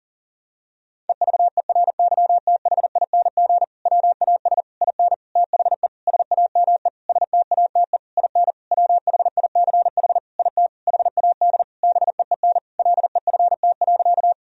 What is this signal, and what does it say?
Morse code, heard as a keyed tone, about 40 wpm, 700 Hz. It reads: EVERYTHING WAS IN THE SAME STATE IN WHICH IT HAD BEEN LEFT.
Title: Sentences from Top 1000 Words - Review Part 2 - 40wpm